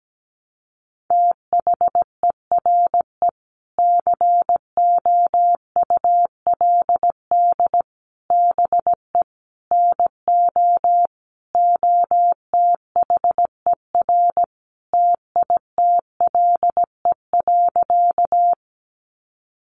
Morse code, heard as a keyed tone, about 17 wpm, 700 Hz. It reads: THERE COULD BE NO OTHER TITLE.
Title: Sentences from Top 1200 Words 17wpm